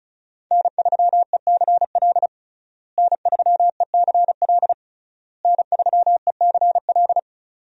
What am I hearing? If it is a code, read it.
Morse code, heard as a keyed tone, about 36 wpm, 700 Hz. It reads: N3ECL N3ECL N3ECL
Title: USA Call Signs - 5 Characters - Repeated 3x - 35wpm